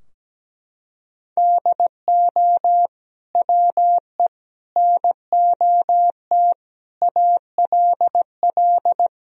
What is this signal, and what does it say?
Morse code, heard as a keyed tone, about 17 wpm, 700 Hz. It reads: DO WE NOT ALL
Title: Sentences from Top 1000 Words - Review Part 3 - 17wpm